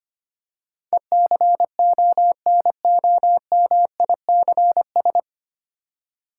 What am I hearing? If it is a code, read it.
Morse code, heard as a keyed tone, about 25 wpm, 700 Hz. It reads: ECONOMICS
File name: practice / 25wpm / 9 Letter Words - First 3 Letters Given - 25wpm